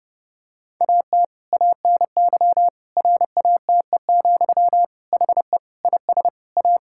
Morse code, heard as a keyed tone, about 30 wpm, 700 Hz. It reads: AT ANY RATE, HE IS A
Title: Sentences from Top 1000 Words - Review Part 5 - 30wpm